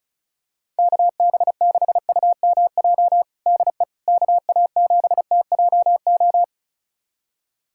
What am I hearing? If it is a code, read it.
Morse code, heard as a keyed tone, about 36 wpm, 700 Hz. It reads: KB6UMJ DE KA7TJO
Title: USA Callsign DE Callsign - 6 Characters - 35wpm